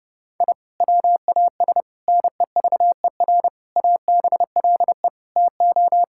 Morse code, heard as a keyed tone, about 30 wpm, 700 Hz. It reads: I WAS NEVER ABLE TO
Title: Sentences from Top 1000 Words - Review Part 6 - 30wpm